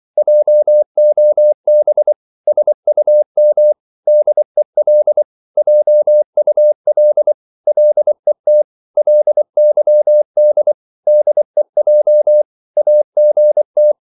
Morse code, heard as a keyed tone, about 24 wpm, 600 Hz. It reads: JOB SUM DEL JUL LET LYD DEJ AGT
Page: Korte danske ord 24wpm